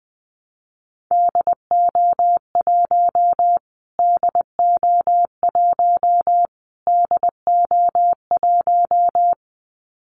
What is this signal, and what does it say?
Morse code, heard as a keyed tone, about 20 wpm, 700 Hz. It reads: DO1 DO1 DO1
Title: Callsign Prefixes for Top 25 DXCC - Repeated 3x - 20wpm